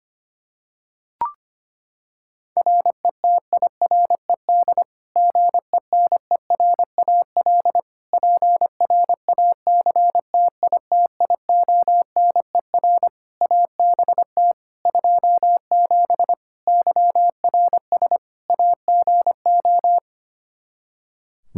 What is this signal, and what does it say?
Morse code, heard as a keyed tone, about 25 wpm, 700 Hz. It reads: RETIRED GENERAL PRACTITIONER ABT 27 YRS AGO
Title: Practice QSO - Exchange 3 - More Info - 25wpm